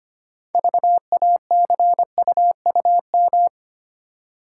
Morse code, heard as a keyed tone, about 25 wpm, 700 Hz. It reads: VACUUM
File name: practice / 25wpm / Easy Words with Hints - 25wpm